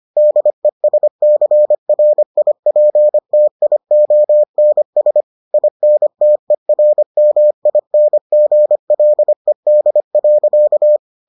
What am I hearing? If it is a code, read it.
Morse code, heard as a keyed tone, about 25 wpm, 600 Hz. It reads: DESCRIPTIONS INTERMINGLED.